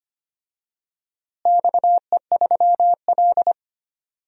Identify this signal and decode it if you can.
Morse code, heard as a keyed tone, about 25 wpm, 700 Hz. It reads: XE3L
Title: DXCC-100 Call Signs - 4 Characters - Repeated 3x - 25wpm